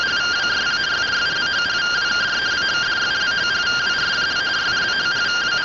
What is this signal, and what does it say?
Two-tone teleprinter signal, about 170 Hz shift.